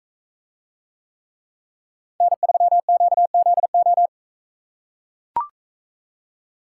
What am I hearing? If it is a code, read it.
Morse code, heard as a keyed tone, about 42 wpm, 700 Hz. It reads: N3QZO